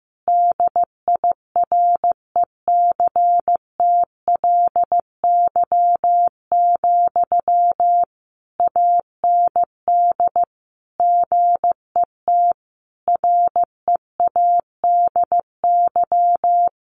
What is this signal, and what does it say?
Morse code, heard as a keyed tone, about 15 wpm, 700 Hz. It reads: DIRECTLY, AND GET READY
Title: Sentences from Top 800 Words 15wpm